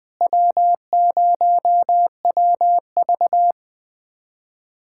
Morse code, heard as a keyed tone, about 21 wpm, 700 Hz. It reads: W0WV
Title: USA Callsign DE Callsign - 4 Characters - 20wpm